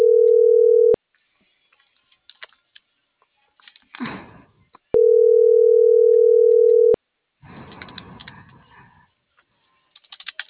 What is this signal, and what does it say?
Ringback tone (North America), 440 + 480 Hz.